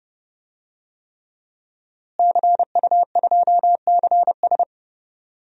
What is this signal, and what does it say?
Morse code, heard as a keyed tone, about 30 wpm, 700 Hz. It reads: CU2CS
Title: DXCC-100 Call Signs - 5 Characters - Repeated 3x - 30wpm